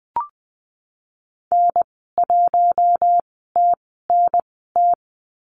Morse code, heard as a keyed tone, about 20 wpm, spaced as for 15 wpm, 700 Hz. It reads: N1TNT
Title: 042 Lesson 8g - Number 1 - Call Signs 20wpm